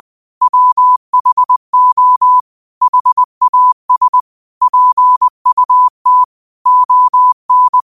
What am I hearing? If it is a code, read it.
Morse code, heard as a keyed tone, about 20 wpm, 1000 Hz. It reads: WHO HAS PUT ON
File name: qotdfiles / 20wpm